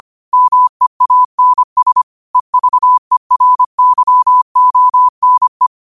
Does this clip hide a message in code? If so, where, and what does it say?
Yes: Morse code from 0.3 to 5.7 s, heard as a keyed tone, about 25 wpm, 1000 Hz. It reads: MEANS EVERYONE